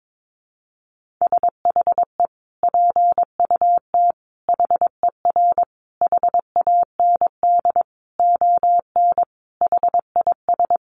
Morse code, heard as a keyed tone, about 22 wpm, 700 Hz. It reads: SHE PUT HER HAND ON HIS
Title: Sentences from Top 1500 Words 22wpm